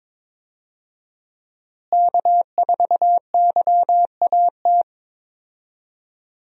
Morse code, heard as a keyed tone, about 22 wpm, 700 Hz. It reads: K4YAT
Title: USA Call Signs - 5 Characters - Repeated 2x - 22wpm